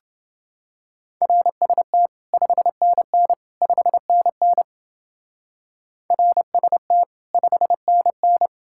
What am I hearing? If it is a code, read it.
Morse code, heard as a keyed tone, about 30 wpm, 700 Hz. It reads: RST 5NN 5NN RST 5NN